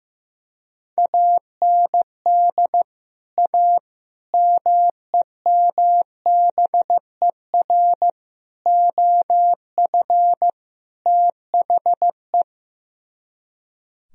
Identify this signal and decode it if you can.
Morse code, heard as a keyed tone, about 15 wpm, 700 Hz. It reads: AND A MEMBER OF THE
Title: Speed-Racing - Sets of 5 Words 15wpm